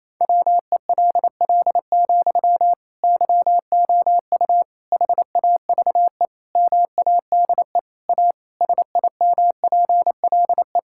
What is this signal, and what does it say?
Morse code, heard as a keyed tone, about 28 wpm, 700 Hz. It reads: WELL, YOU HAVE MADE A SIMPLE
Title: Sentences from Top 1000 Words - Review Part 6 - 28wpm